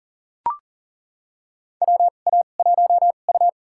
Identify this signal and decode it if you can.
Morse code, heard as a keyed tone, about 40 wpm, spaced as for 29 wpm, 700 Hz. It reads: WA1U
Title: USA Call Signs - 4 Characters - 2x Character Spacing - 40wpm